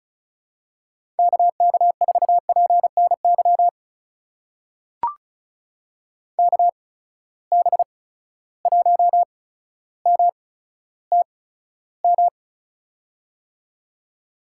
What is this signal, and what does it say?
Morse code, heard as a keyed tone, about 36 wpm, spaced as for 15 wpm, 700 Hz. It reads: KK4PNY KB1MTM
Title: USA Call Signs - 6 Characters - 8x Character Spacing - 35wpm